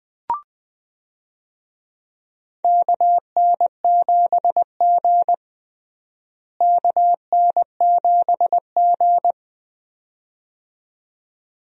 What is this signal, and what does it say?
Morse code, heard as a keyed tone, about 20 wpm, 700 Hz. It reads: KN7G KN7G
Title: USA Call Signs - 4 Characters - Repeated 2x - 20wpm